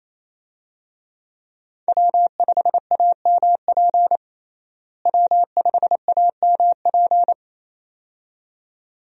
Morse code, heard as a keyed tone, about 27 wpm, 700 Hz. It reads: W5AMP W5AMP
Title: USA Call Signs - 5 Characters - Repeated 2x - 28wpm